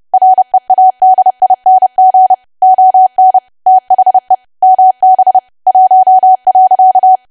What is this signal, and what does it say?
Morse code, heard as a keyed tone, about 30 wpm, 750 Hz. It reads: READING ON THE MB 1.
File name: cwa / W1AW-Level3 / 30WPM